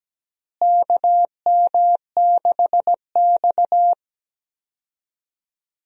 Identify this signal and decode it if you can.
Morse code, heard as a keyed tone, about 17 wpm, 700 Hz. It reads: KM6X
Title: USA Call Signs - 4 Characters - Repeated 3x - 17wpm